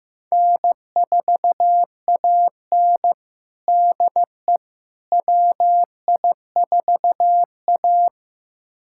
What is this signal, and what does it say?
Morse code, heard as a keyed tone, about 15 wpm, 700 Hz. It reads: N4AN DE WI4A
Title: USA Callsign DE Callsign - 4 Characters - 15wpm